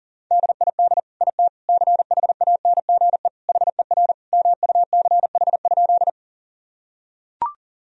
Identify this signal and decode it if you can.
Morse code, heard as a keyed tone, about 37 wpm, 700 Hz. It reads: DID IT CHANGE HER MUCH?
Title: Sentences from Top 1000 Words - Review Part 1 - 40wpm